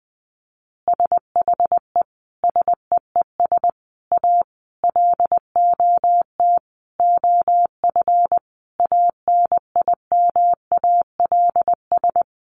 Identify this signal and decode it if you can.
Morse code, heard as a keyed tone, about 20 wpm, 700 Hz. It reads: SHE SEES A LOT OF ANIMALS
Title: Easy Phrases 20wpm